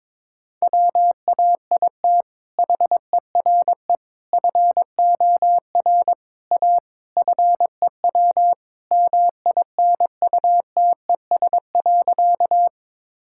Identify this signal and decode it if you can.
Morse code, heard as a keyed tone, about 22 wpm, 700 Hz. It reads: WAIT HERE FOR A FEW MINUTES.
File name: practice / 22wpm / Sentences with 3-Word Phrases - 22wpm